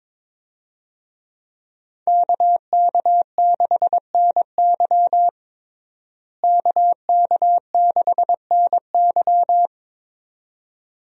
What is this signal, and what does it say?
Morse code, heard as a keyed tone, about 22 wpm, 700 Hz. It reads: KK6NY KK6NY
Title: USA Call Signs - 5 Characters - Repeated 2x - 22wpm